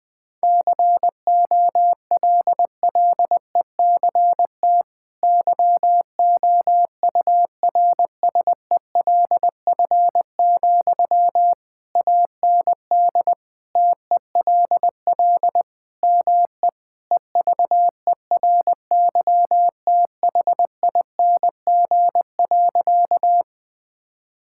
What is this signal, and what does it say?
Morse code, heard as a keyed tone, about 21 wpm, 700 Hz. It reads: COLLECT YOURSELF, AND TELL ME EVERYTHING.
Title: Sentences from Top 1600 Words 20wpm